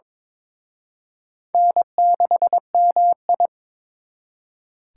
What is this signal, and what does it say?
Morse code, heard as a keyed tone, about 26 wpm, 700 Hz. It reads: N6MI
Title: Speed Racing - USA 1x2 Call Signs 22wpm